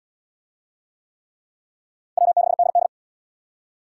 Morse code, heard as a keyed tone, about 67 wpm, 700 Hz. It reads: W6DD